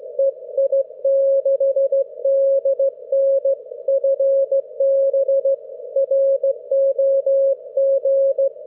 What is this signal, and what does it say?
Morse code, heard as a keyed tone, about 16 wpm, 540 Hz. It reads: EI6DN FB ROG
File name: GW3UEP / EI6DN 10am31May12